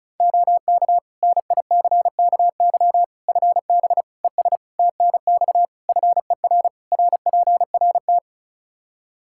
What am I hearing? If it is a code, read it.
Morse code, heard as a keyed tone, about 36 wpm, 700 Hz. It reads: OK NICKY FB ES TNX FER RPRT
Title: Practice QSO - Exchange 2 - Info - 35wpm